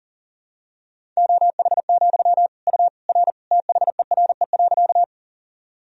Morse code, heard as a keyed tone, about 37 wpm, 700 Hz. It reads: OH, U R THERE.
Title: Sentences from Top 500 Words and Abbreviations 40wpm